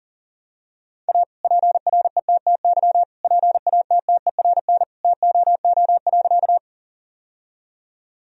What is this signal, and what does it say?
Morse code, heard as a keyed tone, about 40 wpm, 700 Hz. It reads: A PRETTY PATTERN TOO.
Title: Sentences from Top 1000 Words - Review Part 2 - 40wpm